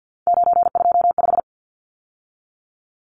Morse code, heard as a keyed tone, about 50 wpm, 700 Hz. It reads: 925
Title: Numbers - 3-Digit 50wpm